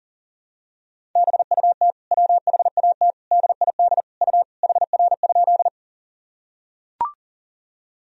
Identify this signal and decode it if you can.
Morse code, heard as a keyed tone, about 40 wpm, 700 Hz. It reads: BUT WHAT DID U HR?
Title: Sentences from Top 300 Words and Abbreviations 40wpm